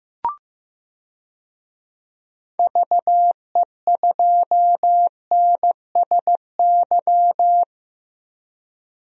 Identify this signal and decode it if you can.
Morse code, heard as a keyed tone, about 15 wpm, 700 Hz. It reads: VE2NSY
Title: DXCC-100 Call Signs - 6 Characters - 15wpm